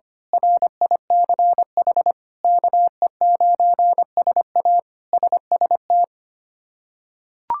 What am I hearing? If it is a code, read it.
Morse code, heard as a keyed tone, about 25 wpm, 700 Hz. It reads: RICH KE9SA SST
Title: K1USN SST QSO Practice - 25wpm